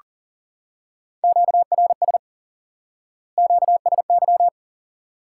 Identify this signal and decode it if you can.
Morse code, heard as a keyed tone, about 40 wpm, 700 Hz. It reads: QRS QSY
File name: IQE Course 3 - Lesson 89 - YAGI - Two Elements - 40wpm